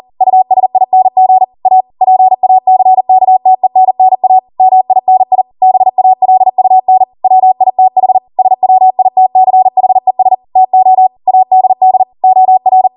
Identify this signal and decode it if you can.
Morse code, heard as a keyed tone, about 40 wpm, 750 Hz. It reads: USING A PACKTENNA MINI BALUN WITH SWITCHES TO ADD OR
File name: Morse / 40WPM